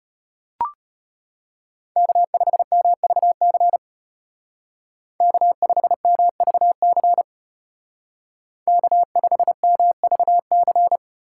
Morse code, heard as a keyed tone, about 30 wpm, 700 Hz. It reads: K5MVC K5MVC K5MVC
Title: Speed Racing - USA Call Signs 25wpm